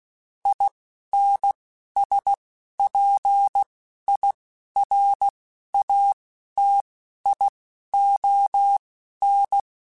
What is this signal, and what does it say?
Morse code, heard as a keyed tone, about 16 wpm, spaced as for 11 wpm, 800 Hz. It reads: INSPIRATION
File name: Quotes Spaced / Quotes Spaced-mp3s / Persperation sp @15